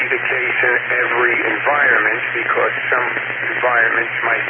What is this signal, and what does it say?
Emergency Alert System (SAME) tones, heard as harsh buzzing bursts, with speech. It is heard from 0.3 to 3.5 s.